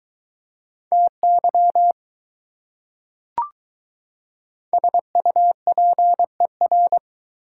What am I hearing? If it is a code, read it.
Morse code, heard as a keyed tone, about 22 wpm, 700 Hz. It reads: TY SUPER